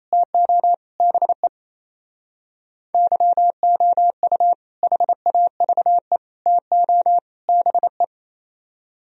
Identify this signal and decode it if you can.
Morse code, heard as a keyed tone, about 34 wpm, 700 Hz. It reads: TO BE YOU HAVE TO BE
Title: Speed-Racing - Sets of 4 Words 28wpm